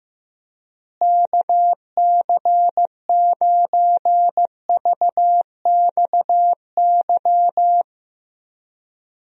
Morse code, heard as a keyed tone, about 15 wpm, 700 Hz. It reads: KC9VXY
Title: USA Call Signs - 6 Characters - Repeated 2x - 15wpm